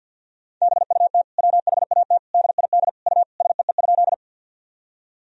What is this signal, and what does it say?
Morse code, heard as a keyed tone, about 48 wpm, 700 Hz. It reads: BUT WHAT DID U SEE?
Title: Sentences from Top 100 Words and Abbreviations 50wpm